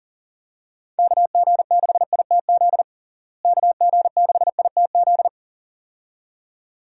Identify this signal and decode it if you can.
Morse code, heard as a keyed tone, about 37 wpm, 700 Hz. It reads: KG6ITZ KG6ITZ
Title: USA Call Signs - 6 Characters - Repeated 2x - 40wpm